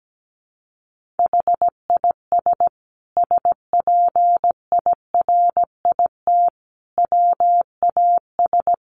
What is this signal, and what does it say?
Morse code, heard as a keyed tone, about 17 wpm, 700 Hz. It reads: HIS SPIRIT WAS
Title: Sentences from Top 1400 Words 17wpm